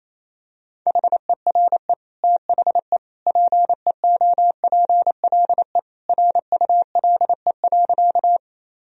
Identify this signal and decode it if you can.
Morse code, heard as a keyed tone, about 28 wpm, 700 Hz. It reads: HERE THE PEOPLE RULE.
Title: Sentences from Top 1000 Words - Review Part 2 - 28wpm